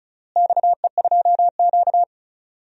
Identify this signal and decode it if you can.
Morse code, heard as a keyed tone, about 34 wpm, 700 Hz. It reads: XE2Q
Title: DXCC-100 Call Signs - 4 Characters - Repeated 3x - 35wpm